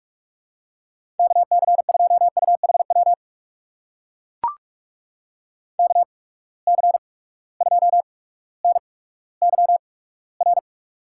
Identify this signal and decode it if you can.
Morse code, heard as a keyed tone, about 44 wpm, 690 Hz. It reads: K C 2 U H W K C 2 N Y R
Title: USA Call Signs - 8x Character Spacing - 45wpm